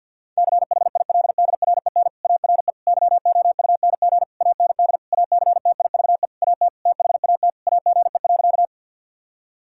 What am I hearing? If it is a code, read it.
Morse code, heard as a keyed tone, about 50 wpm, 700 Hz. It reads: CHILDREN ARE YOUNG AND ACTIVE AT THAT AGE.
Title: Sentences from Top 1000 Words - Review Part 6 - 50wpm